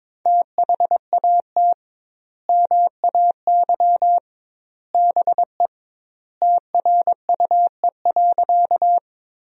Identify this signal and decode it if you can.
Morse code, heard as a keyed tone, about 22 wpm, 700 Hz. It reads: THAT MAY BE TRUE.
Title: 2x Word-Spacing - Sentences from Top 400 Words - 22wpm